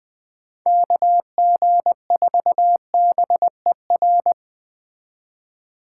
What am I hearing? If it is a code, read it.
Morse code, heard as a keyed tone, about 21 wpm, 700 Hz. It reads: KG4BER
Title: Speed Racing - USA Call Signs 17wpm